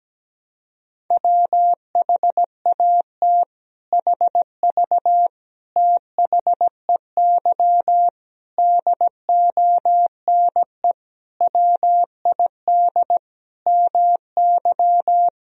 Morse code, heard as a keyed tone, about 17 wpm, 700 Hz. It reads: WHAT HV THEY DONE WID MY
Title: Sentences from Top 500 Words and Abbreviations 17wpm